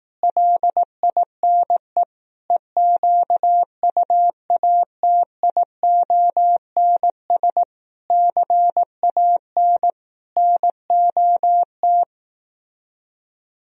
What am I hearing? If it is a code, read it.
Morse code, heard as a keyed tone, about 18 wpm, 700 Hz. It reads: LINE EQUATIONS CAN NOT
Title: Speed-Racing - Sets of 5 Words 15wpm